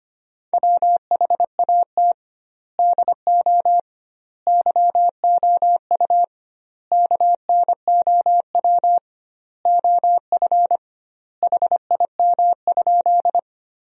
Morse code, heard as a keyed tone, about 25 wpm, 700 Hz. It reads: WHAT DO YOU KNOW OF HIM?
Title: 2x Word-Spacing - Sentences from Top 100 Words - 25wpm